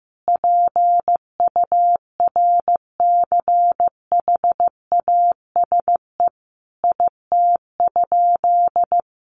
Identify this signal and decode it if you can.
Morse code, heard as a keyed tone, about 15 wpm, 700 Hz. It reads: PURCHASE IT?
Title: Sentences from Top 800 Words 15wpm